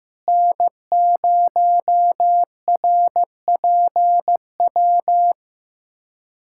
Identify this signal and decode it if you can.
Morse code, heard as a keyed tone, about 15 wpm, 700 Hz. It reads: N0RPW
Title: USA Callsign DE Callsign - 5 Characters - 15wpm